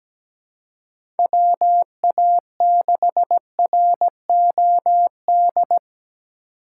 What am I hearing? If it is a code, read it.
Morse code, heard as a keyed tone, about 17 wpm, 700 Hz. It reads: WA6ROD
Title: USA Call Signs - 2x Character Spacing - 17wpm